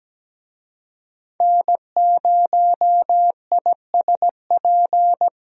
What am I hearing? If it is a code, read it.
Morse code, heard as a keyed tone, about 17 wpm, 700 Hz. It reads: N0ISP
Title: USA Call Signs - 5 Characters - Repeated 3x - 17wpm